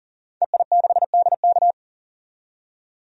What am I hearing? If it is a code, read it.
Morse code, heard as a keyed tone, about 40 wpm, 700 Hz. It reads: EI6DK
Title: DXCC-100 Call Signs - 5 Characters - 40wpm